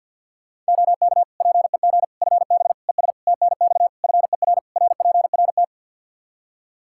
Morse code, heard as a keyed tone, about 52 wpm, 700 Hz. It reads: OK PEG FB ES TNX FER RPRT
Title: Practice QSO - Exchange 2 - Info - 50wpm